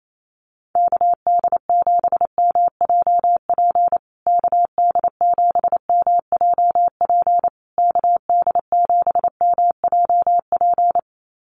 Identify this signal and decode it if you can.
Morse code, heard as a keyed tone, about 28 wpm, 700 Hz. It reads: KD7MJP KD7MJP KD7MJP
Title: USA Call Signs Repeated 3x 28wpm